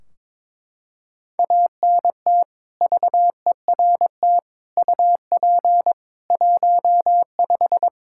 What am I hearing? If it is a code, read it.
Morse code, heard as a keyed tone, about 22 wpm, 700 Hz. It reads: ANT VERT UP 15
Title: Practice QSO - Exchange 2 - Info - 22wpm